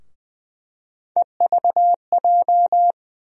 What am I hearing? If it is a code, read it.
Morse code, heard as a keyed tone, about 20 wpm, 700 Hz. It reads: EVJ
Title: USA Call Sign Postfixes - 3 Letters 20wpm